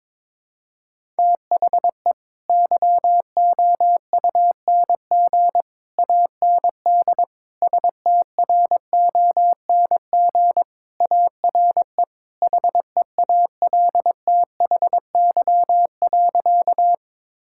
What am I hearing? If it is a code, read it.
Morse code, heard as a keyed tone, about 22 wpm, 700 Hz. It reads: THE YOUNG AND STRONG ARE HEALTHY.
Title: Sentences from Top 1000 Words - Review Part 5 - 22wpm